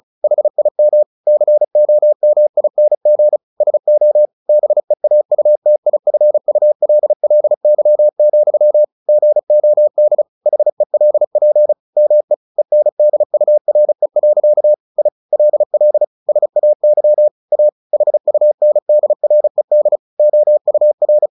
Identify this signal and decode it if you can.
Morse code, heard as a keyed tone, about 34 wpm, 600 Hz. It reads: HIM COMING SO BEAUTIFULLY, GOD HELP ME ENDURE. I LL SAY A HUNDRED OUR
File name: the old man and the sea 2 35wpm